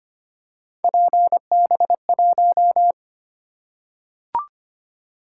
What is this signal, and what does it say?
Morse code, heard as a keyed tone, about 25 wpm, 700 Hz. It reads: PB1
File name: Three Letter-Number Combinations - 25wpm